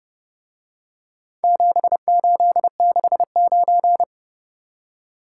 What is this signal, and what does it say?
Morse code, heard as a keyed tone, about 29 wpm, 700 Hz. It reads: 7869